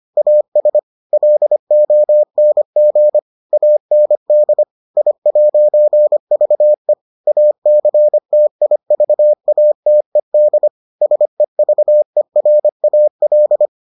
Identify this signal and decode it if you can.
Morse code, heard as a keyed tone, about 25 wpm, 600 Hz. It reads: AS LONG AND I'VE ACTIVATED SEVERAL